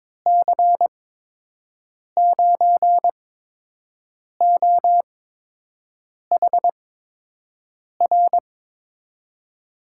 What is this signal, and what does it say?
Morse code, heard as a keyed tone, about 22 wpm, spaced as for 6 wpm, 700 Hz. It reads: C9OHR
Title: USA Call Signs - 6 Characters - 8x Character Spacing - 22wpm